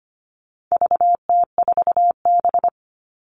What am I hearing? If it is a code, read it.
Morse code, heard as a keyed tone, about 25 wpm, 700 Hz. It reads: VT4B
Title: All Cut Numbers - 4-Digit 25wpm